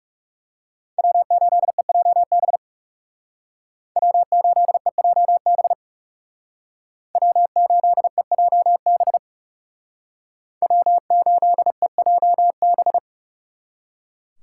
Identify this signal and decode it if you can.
Morse code, heard as a keyed tone, about 38 wpm, 700 Hz. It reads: W8EJB W8EJB W8EJB W8EJB